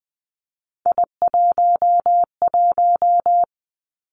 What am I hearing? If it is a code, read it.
Morse code, heard as a keyed tone, about 20 wpm, 700 Hz. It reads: I11
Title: Three Challenging Letter-Number Combinations 20wpm